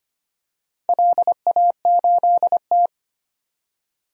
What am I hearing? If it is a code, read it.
Morse code, heard as a keyed tone, about 25 wpm, 700 Hz. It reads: LA8T
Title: DXCC-100 Call Signs - 4 Characters - 25wpm